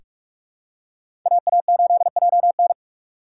Morse code, heard as a keyed tone, about 42 wpm, 690 Hz. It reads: AA8JN
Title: USA Call Signs - 5 Characters - 8x Character Spacing - 45wpm